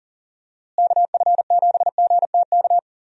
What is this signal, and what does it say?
Morse code, heard as a keyed tone, about 40 wpm, 700 Hz. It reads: KF7GTK
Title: USA Call Signs - 6 Characters - Repeated 2x - 40wpm